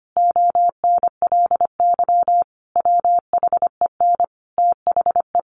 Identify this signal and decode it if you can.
Morse code, heard as a keyed tone, about 25 wpm, 700 Hz. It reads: ONLY WHEN THE
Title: Sentences from Top 1300 Words 25wpm